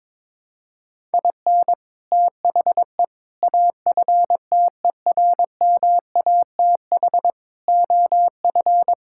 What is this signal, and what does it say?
Morse code, heard as a keyed tone, about 22 wpm, 700 Hz. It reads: IN THE AFTERMATH OF
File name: practice / 22wpm / Build A Phrase - 5-Word Phrases - 22wpm